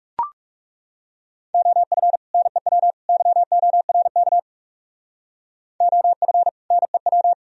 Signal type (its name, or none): Morse code